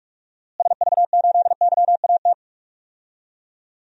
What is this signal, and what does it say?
Morse code, heard as a keyed tone, about 45 wpm, 700 Hz. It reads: SV8YAT
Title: DXCC-100 Call Signs - 6 Characters - Repeated 2x - 45wpm